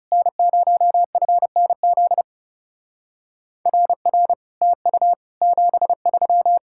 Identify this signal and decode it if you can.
Morse code, heard as a keyed tone, about 36 wpm, 700 Hz. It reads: N0FNZ RR TU 73
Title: Speed-Racing Common QSO Phrases Encore - 30wpm